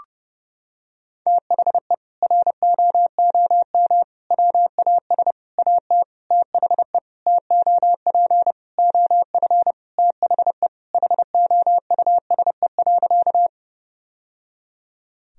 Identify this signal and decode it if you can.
Morse code, heard as a keyed tone, about 32 wpm, 700 Hz. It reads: THE ROOM WAS AT THE TOP OF THE HOUSE.
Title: Sentences from Top 1000 Words - Review Part 2 - 30wpm